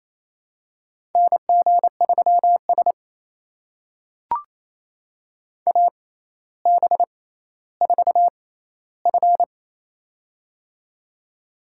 Morse code, heard as a keyed tone, about 28 wpm, spaced as for 14 wpm, 700 Hz. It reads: NG3S AB4F
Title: USA Call Signs - 4 Characters - 6x Character Spacing - 28wpm